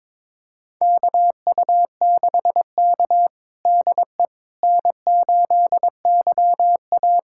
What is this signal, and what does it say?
Morse code, heard as a keyed tone, about 22 wpm, 700 Hz. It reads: KU6K DE N8YA
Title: USA Callsign DE Callsign - 4 Characters - 22wpm